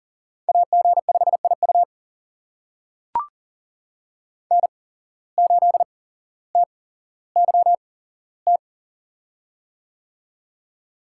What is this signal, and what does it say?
Morse code, heard as a keyed tone, about 40 wpm, spaced as for 17 wpm, 700 Hz. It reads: AG5IU N8TYT
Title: USA Call Signs - 5 Characters - 8x Character Spacing - 40wpm